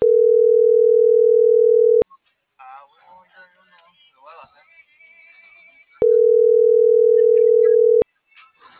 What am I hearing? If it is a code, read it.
Ringback tone (North America), 440 + 480 Hz.